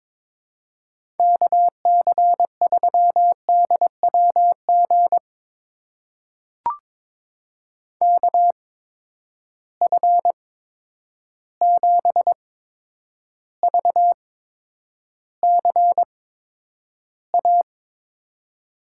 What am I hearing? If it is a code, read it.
Morse code, heard as a keyed tone, about 22 wpm, spaced as for 9 wpm, 700 Hz. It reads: KC3DWG KF7VCA
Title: USA Call Signs - 8x Character Spacing - 22wpm